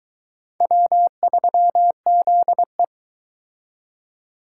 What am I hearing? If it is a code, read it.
Morse code, heard as a keyed tone, about 22 wpm, 700 Hz. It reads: W3ZE